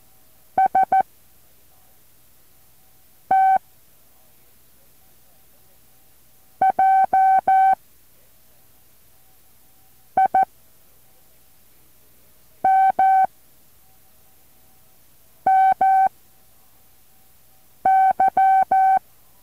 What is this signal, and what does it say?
Morse code, heard as a keyed tone, about 14 wpm, 770 Hz. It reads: S T J I M M Y
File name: intl morse code2